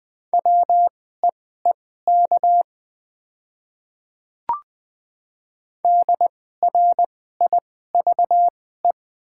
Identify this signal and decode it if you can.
Morse code, heard as a keyed tone, about 20 wpm, spaced as for 14 wpm, 700 Hz. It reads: WEEK DRIVE
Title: 2x Character Spacing - Top 500 Words - 20wpm